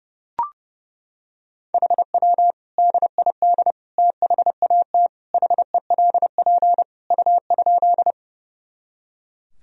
Morse code, heard as a keyed tone, about 32 wpm, 700 Hz. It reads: HW DID THAT HELP U?
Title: Sentences from Top 200 Words and Abbreviations 30wpm